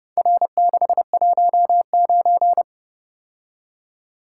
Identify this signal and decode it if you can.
Morse code, heard as a keyed tone, about 30 wpm, 700 Hz. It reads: R619